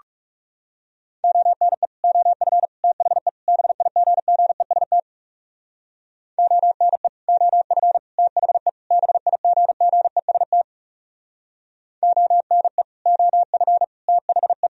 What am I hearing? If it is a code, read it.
Morse code, heard as a keyed tone, about 40 wpm, 700 Hz. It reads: ONE OF THE BIGGEST ONE OF THE BIGGEST ONE OF THE